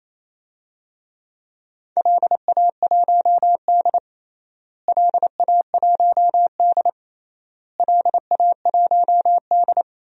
Morse code, heard as a keyed tone, about 28 wpm, 700 Hz. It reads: LA1D LA1D LA1D
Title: DXCC-100 Call Signs - 4 Characters - Repeated 3x - 28wpm